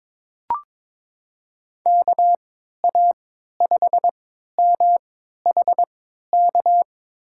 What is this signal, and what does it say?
Morse code, heard as a keyed tone, about 23 wpm, spaced as for 13 wpm, 700 Hz. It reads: KA5MHK
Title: USA Call Signs - 6 Characters - 3x Character Spacing - 22wpm